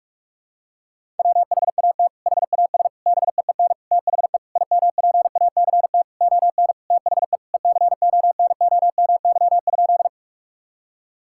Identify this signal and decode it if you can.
Morse code, heard as a keyed tone, about 46 wpm, 700 Hz. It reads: WHAT HAS BEEN THE IMPACT ON THE ECONOMY?
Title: Sentences from Top 1000 Words - Review Part 6 - 45wpm